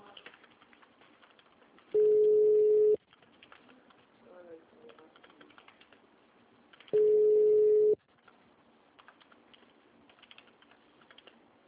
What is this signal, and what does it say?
Ringback tone (Europe, 425 Hz).